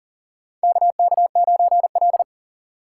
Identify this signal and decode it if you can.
Morse code, heard as a keyed tone, about 40 wpm, 700 Hz. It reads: KK9L
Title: USA Call Signs - 2x Character Spacing - 40wpm